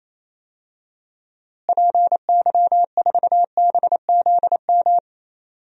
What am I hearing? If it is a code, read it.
Morse code, heard as a keyed tone, about 28 wpm, 700 Hz. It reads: PY4BZM
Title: DXCC-100 Call Signs - 6 Characters - Repeated 3x - 28wpm